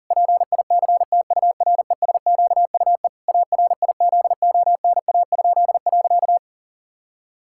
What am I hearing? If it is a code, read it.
Morse code, heard as a keyed tone, about 40 wpm, 700 Hz. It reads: PICTURESQUE ARIZONA?.
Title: Pangram Challenge v2 - 40wpm